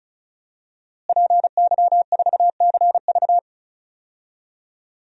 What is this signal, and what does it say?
Morse code, heard as a keyed tone, about 36 wpm, 700 Hz. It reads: PY4CV
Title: DXCC-100 Call Signs - 5 Characters - Repeated 2x - 35wpm